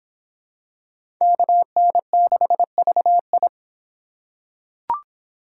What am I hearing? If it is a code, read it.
Morse code, heard as a keyed tone, about 25 wpm, 700 Hz. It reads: KN6VI